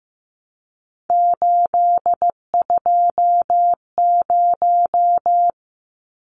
Morse code, heard as a keyed tone, about 15 wpm, 700 Hz. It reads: 820
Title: Numbers - 3-Digit 15wpm